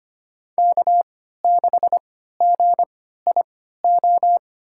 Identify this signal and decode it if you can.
Morse code, heard as a keyed tone, about 25 wpm, spaced as for 14 wpm, 700 Hz. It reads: K6GIO
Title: USA Call Signs - 3x Character Spacing - 25wpm